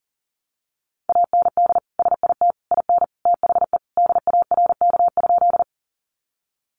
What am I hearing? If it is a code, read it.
Morse code, heard as a keyed tone, about 40 wpm, 700 Hz. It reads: AND SIT IN THE DARK?
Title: Sentences from Top 500 Words 40wpm